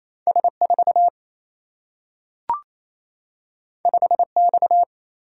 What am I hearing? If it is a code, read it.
Morse code, heard as a keyed tone, about 28 wpm, 700 Hz. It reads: S 4 5 X
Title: Single-Letter Single-Number 28wpm